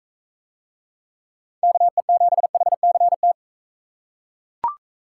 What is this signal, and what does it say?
Morse code, heard as a keyed tone, about 40 wpm, 700 Hz. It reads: KE7HCT